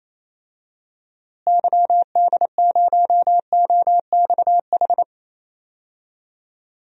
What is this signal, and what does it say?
Morse code, heard as a keyed tone, about 28 wpm, 700 Hz. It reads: YD0OXH
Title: DXCC-100 Call Signs - 6 Characters - Repeated 2x - 28wpm